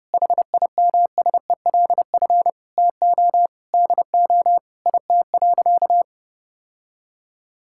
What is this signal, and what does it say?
Morse code, heard as a keyed tone, about 30 wpm, 700 Hz. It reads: HIMSELF TO DO IT.
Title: Sentences from Top 1600 Words 30wpm